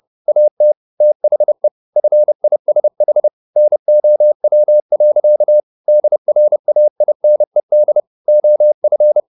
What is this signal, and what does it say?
Morse code, heard as a keyed tone, about 30 wpm, 600 Hz. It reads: AT THE FISH NOW. DRAINED OF